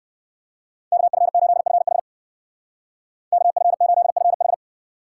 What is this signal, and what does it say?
Morse code, heard as a keyed tone, about 57 wpm, 700 Hz. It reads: KF8FH KF8FH